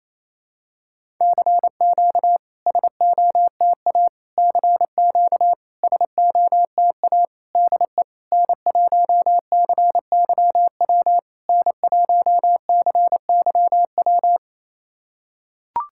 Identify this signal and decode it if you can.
Morse code, heard as a keyed tone, about 27 wpm, 690 Hz. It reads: CQ SOTA CQ SOTA DE N1CYW N1CYW
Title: SOTA QSO Practice 28wpm